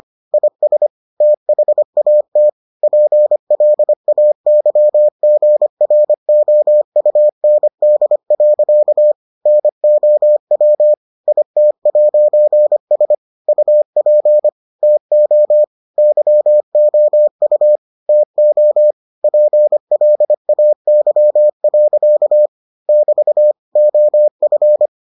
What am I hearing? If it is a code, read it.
Morse code, heard as a keyed tone, about 25 wpm, 600 Hz. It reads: IS THAT PLAYGROUND. NOW IT'S UP TO YOU TO PLAY. <BT> OF